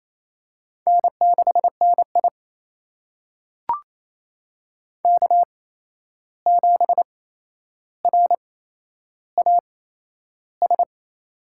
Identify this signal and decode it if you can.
Morse code, heard as a keyed tone, about 28 wpm, spaced as for 11 wpm, 700 Hz. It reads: N6NI K7RAS
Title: USA Call Signs - 8x Character Spacing - 28wpm